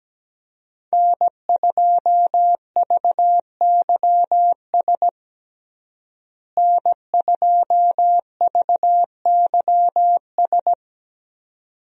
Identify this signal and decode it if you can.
Morse code, heard as a keyed tone, about 17 wpm, 700 Hz. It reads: N2VYS N2VYS
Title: USA Call Signs - 5 Characters - Repeated 2x - 17wpm